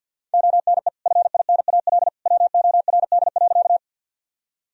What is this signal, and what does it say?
Morse code, heard as a keyed tone, about 46 wpm, 700 Hz. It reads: ONE FINAL WORD.
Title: Sentences from Top 1000 Words - Review Part 2 - 50wpm